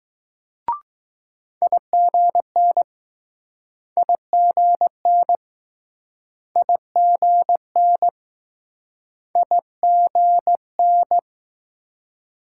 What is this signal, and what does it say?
Morse code, heard as a keyed tone, about 20 wpm, 700 Hz. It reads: IGN IGN IGN IGN